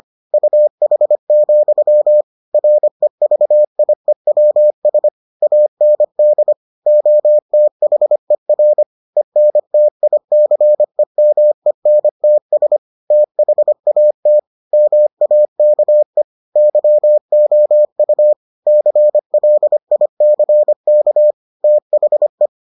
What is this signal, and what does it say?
Morse code, heard as a keyed tone, about 25 wpm, 600 Hz. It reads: UH, REVIEWS AND OTHER ENTICEMENTS THAT MAKE YOU CLICK THE